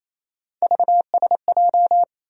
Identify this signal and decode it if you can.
Morse code, heard as a keyed tone, about 28 wpm, 700 Hz. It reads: VSJ
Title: USA Call Sign Postfixes - 3 Letters 28wpm